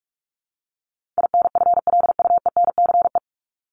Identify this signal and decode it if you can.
Morse code, heard as a keyed tone, about 44 wpm, 700 Hz. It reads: INFLUENCE
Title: Top 1001-1250 Words 45wpm